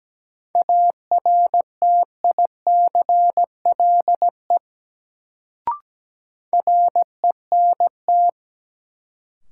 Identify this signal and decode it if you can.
Morse code, heard as a keyed tone, about 17 wpm, 700 Hz. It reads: ARTICLE RENT
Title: Sentences from Top 1000 Words - Review Part 5 - New Word Drill - 17wpm